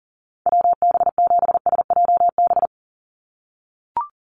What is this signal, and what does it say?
Morse code, heard as a keyed tone, about 40 wpm, 700 Hz. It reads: WB7SJB
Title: USA Call Signs 40wpm